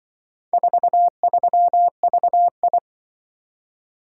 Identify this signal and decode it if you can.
Morse code, heard as a keyed tone, about 25 wpm, 700 Hz. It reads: 43VI
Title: Speed-Racing - Four Letter-Number Combinations - 20wpm